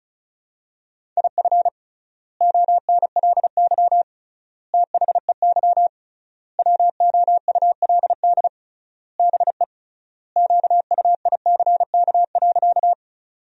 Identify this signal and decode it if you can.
Morse code, heard as a keyed tone, about 34 wpm, 700 Hz. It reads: IF ONLY THEY WOULD BE QUICK.
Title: 3x Word-Spacing - Sentences from Top 500 Words - 35wpm